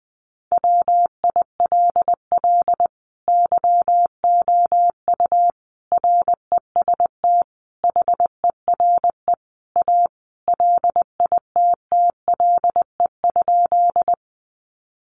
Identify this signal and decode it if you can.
Morse code, heard as a keyed tone, about 20 wpm, 700 Hz. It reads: WILL YOU REST HERE A LITTLE?
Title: Sentences from Top 500 Words 20wpm